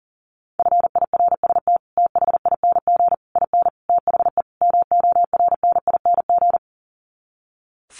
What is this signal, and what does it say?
Morse code, heard as a keyed tone, about 40 wpm, 700 Hz. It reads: FIRST THING IN THE MORNING
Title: Sets of 5 Words Encore2 40wpm